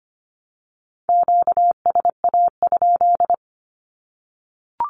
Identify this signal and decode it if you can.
Morse code, heard as a keyed tone, about 25 wpm, 700 Hz. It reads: QSA?
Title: Q-Codes 25wpm